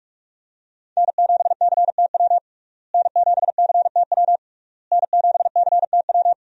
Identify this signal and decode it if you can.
Morse code, heard as a keyed tone, about 46 wpm, 700 Hz. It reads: N7CTW N7CTW N7CTW
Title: USA Call Signs - 5 Characters - Repeated 3x - 45wpm